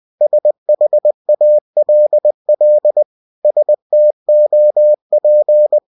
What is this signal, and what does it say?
Morse code, heard as a keyed tone, about 20 wpm, 600 Hz. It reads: SHALL STOP
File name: cw-books / winnie-the-pooh / winnie-20wpm-page-0101